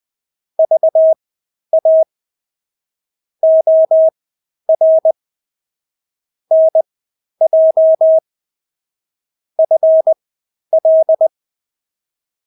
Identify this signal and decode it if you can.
Morse code, heard as a keyed tone, about 20 wpm, spaced as for 11 wpm, 640 Hz. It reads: VA OR NJ FL
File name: State-Abbreviations-10